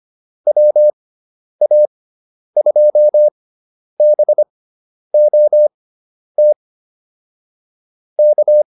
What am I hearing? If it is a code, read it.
Morse code, heard as a keyed tone, about 25 wpm, spaced as for 10 wpm, 600 Hz. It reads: WA2BOT K